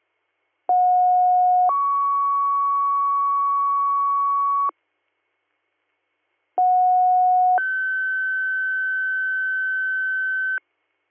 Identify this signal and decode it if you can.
Two-tone page, heard as a one-second tone, then a three-second tone, about 730 then 1120 Hz, about 730 then 1550 Hz. It is heard from 0.7 to 10.6 s.